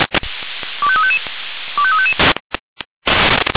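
Five-tone selective call (ZVEI), heard as quick runs of five tones, tones of about 70 ms. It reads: 25355 25355
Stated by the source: ZVEI five-tone selective call received on SDR.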